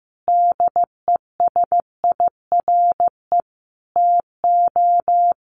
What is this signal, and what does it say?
Morse code, heard as a keyed tone, about 14 wpm, 700 Hz. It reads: DESIRE TO
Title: Sentences from Top 1100 Words 15wpm